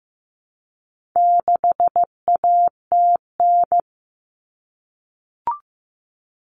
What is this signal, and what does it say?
Morse code, heard as a keyed tone, about 15 wpm, 700 Hz. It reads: 6ATN
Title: All Cut Numbers - 4-Digit 15wpm